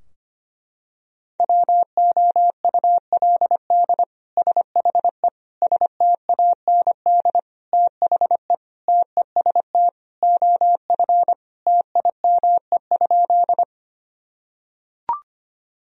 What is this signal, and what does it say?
Morse code, heard as a keyed tone, about 25 wpm, 700 Hz. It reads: WOULD SHE STAND THE TEST OF TIME?
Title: Sentences from Top 1000 Words - Review Part 3 - 25wpm